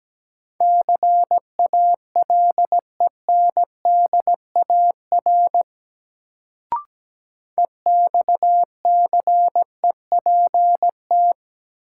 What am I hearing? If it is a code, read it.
Morse code, heard as a keyed tone, about 17 wpm, 700 Hz. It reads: CALENDAR EXCEPT
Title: Sentences from Top 1600 Words - New Word Drill - 17wpm